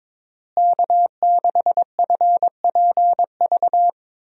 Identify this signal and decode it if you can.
Morse code, heard as a keyed tone, about 22 wpm, 700 Hz. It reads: K6FPV
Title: USA Call Signs - 5 Characters - Repeated 3x - 22wpm